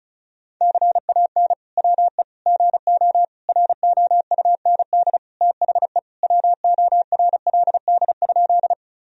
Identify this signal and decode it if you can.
Morse code, heard as a keyed tone, about 36 wpm, 700 Hz. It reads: CAN WE GO ROUND THE WORLD?
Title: Sentences from Top 1000 Words - Review Part 1 - 35wpm